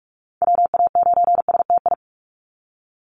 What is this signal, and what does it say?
Morse code, heard as a keyed tone, about 46 wpm, 700 Hz. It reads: WA9STI
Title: NA SOTA Chaser Callsigns 45wpm